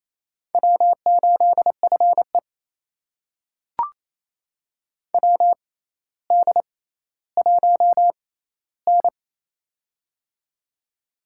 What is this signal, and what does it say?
Morse code, heard as a keyed tone, about 28 wpm, spaced as for 14 wpm, 700 Hz. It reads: W8FE WD1N
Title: USA Call Signs - 4 Characters - 6x Character Spacing - 28wpm